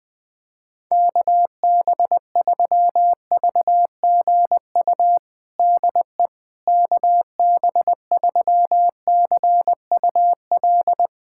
Morse code, heard as a keyed tone, about 20 wpm, 700 Hz. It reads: KB3VGU DE KB3CUL
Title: USA Callsign DE Callsign 20wpm